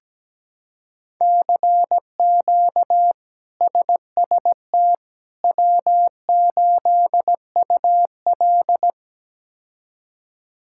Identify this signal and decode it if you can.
Morse code, heard as a keyed tone, about 17 wpm, 700 Hz. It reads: CQ SST W8UL
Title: K1USN SST QSO Practice - 17wpm